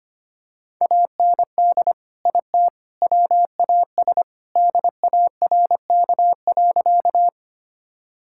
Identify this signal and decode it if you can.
Morse code, heard as a keyed tone, about 25 wpm, 700 Hz. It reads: AND IT WAS DARK.
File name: practice / 25wpm / Sentences with 3-Word Phrases - 25wpm